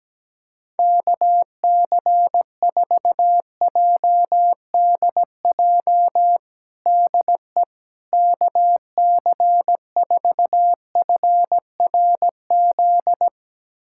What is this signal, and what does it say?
Morse code, heard as a keyed tone, about 17 wpm, 700 Hz. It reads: KC4JDJ DE KC4FRZ
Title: USA Callsign DE Callsign - 6 Characters - 17wpm